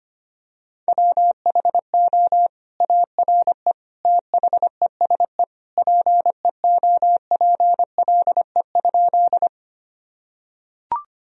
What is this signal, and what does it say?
Morse code, heard as a keyed tone, about 25 wpm, 700 Hz. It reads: WHO ARE THESE PEOPLE?
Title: Sentences from Top 1000 Words - Review Part 1 - 25wpm